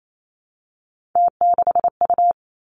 Morse code, heard as a keyed tone, about 28 wpm, 690 Hz. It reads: T6U
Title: Three Challenging Letter-Number Combinations 28wpm